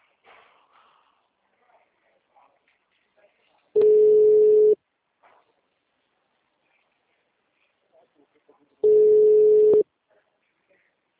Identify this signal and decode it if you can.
Ringback tone (Europe, 425 Hz), with speech.
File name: out- -2011- - - .55403